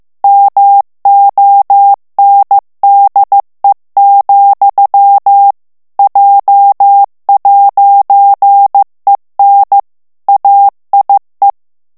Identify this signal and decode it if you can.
Morse code, heard as a keyed tone, about 15 wpm, 800 Hz. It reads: MONDE, J'EN AIE
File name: morse mp3 / FRANCAIS TEXTE CLAIR 2 VITESSE 14 MOTS MINUTES